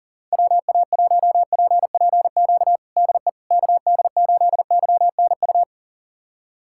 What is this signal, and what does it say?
Morse code, heard as a keyed tone, about 37 wpm, 700 Hz. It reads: WA1PPQ DE KD8YNU
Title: USA Callsign DE Callsign - 6 Characters - 40wpm